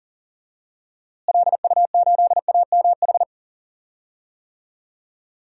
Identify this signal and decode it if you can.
Morse code, heard as a keyed tone, about 40 wpm, 700 Hz. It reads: LU8AMH
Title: DXCC-100 Call Signs - 6 Characters - Repeated 2x - 40wpm